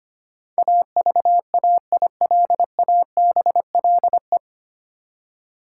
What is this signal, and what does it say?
Morse code, heard as a keyed tone, about 25 wpm, 700 Hz. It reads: AVAILABLE
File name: Nine Letter Words - 25wpm